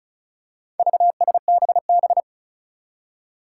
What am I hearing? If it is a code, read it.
Morse code, heard as a keyed tone, about 34 wpm, 700 Hz. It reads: VSBB
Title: Four Challenging Letter-Number Combinations 35wpm